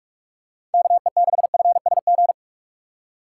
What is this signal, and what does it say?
Morse code, heard as a keyed tone, about 44 wpm, 700 Hz. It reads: KE6FSG
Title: USA Call Signs - 4x Character Spacing - 45wpm